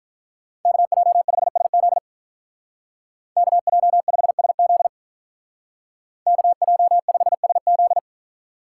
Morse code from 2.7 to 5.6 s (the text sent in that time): KJ5SZ